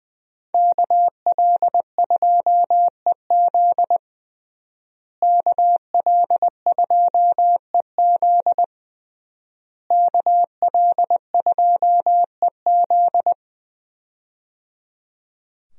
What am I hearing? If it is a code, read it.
Morse code, heard as a keyed tone, about 21 wpm, 700 Hz. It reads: KL2EZ KL2EZ KL2EZ
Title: USA Call Signs - 5 Characters - Repeated 3x - 20wpm